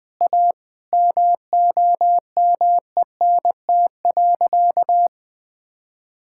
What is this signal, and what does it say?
Morse code, heard as a keyed tone, about 20 wpm, 700 Hz. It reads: A MOMENT.
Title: Sentences with 3-Word Phrases - 20wpm